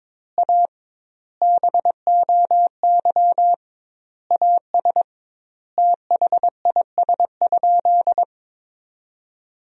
Morse code, heard as a keyed tone, about 22 wpm, 700 Hz. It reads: A BOY AS THIS?
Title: 2x Word-Spacing - Sentences from Top 400 Words - 22wpm